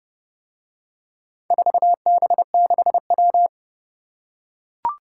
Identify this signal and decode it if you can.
Morse code, heard as a keyed tone, about 30 wpm, 700 Hz. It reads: 4B6W